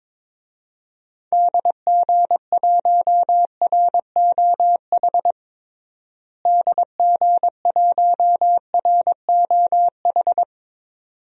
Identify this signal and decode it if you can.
Morse code, heard as a keyed tone, about 22 wpm, 700 Hz. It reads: DG1ROH DG1ROH
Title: DXCC-100 Call Signs - 6 Characters - Repeated 2x - 22wpm